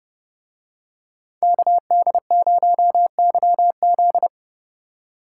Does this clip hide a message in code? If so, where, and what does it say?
Yes: Morse code from 1.4 to 4.3 s, heard as a keyed tone, about 31 wpm, 700 Hz. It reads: KD0YZ